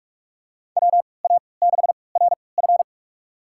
Morse code, heard as a keyed tone, about 46 wpm, spaced as for 26 wpm, 700 Hz. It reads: WA6RF
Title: USA Call Signs - 5 Characters - 3x Character Spacing - 45wpm